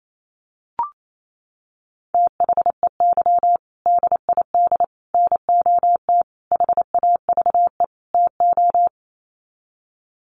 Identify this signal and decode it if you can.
Morse code, heard as a keyed tone, about 28 wpm, 700 Hz. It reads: THEY DID NOT HAVE TO
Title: Sets of 5 Words Encore2 28wpm